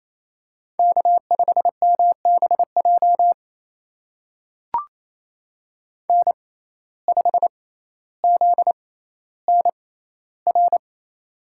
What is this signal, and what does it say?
Morse code, heard as a keyed tone, about 28 wpm, spaced as for 14 wpm, 700 Hz. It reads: K5MBJ N5ZNR
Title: USA Call Signs - 5 Characters - 6x Character Spacing - 28wpm